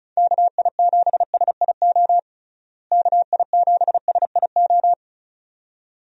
Morse code, heard as a keyed tone, about 34 wpm, 700 Hz. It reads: KI7SIO KI7SIO
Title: USA Call Signs - 6 Characters - Repeated 2x - 35wpm